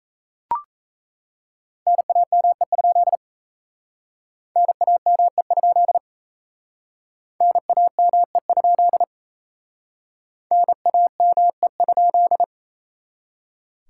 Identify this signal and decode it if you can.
Morse code, heard as a keyed tone, about 35 wpm, 700 Hz. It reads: NAME? NAME? NAME? NAME?